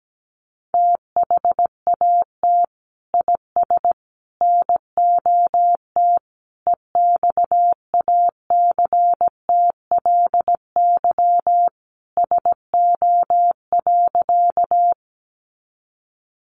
Morse code, heard as a keyed tone, about 17 wpm, 700 Hz. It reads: THAT IS NOT EXACTLY SO.
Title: Sentences from Top 900 Words 17wpm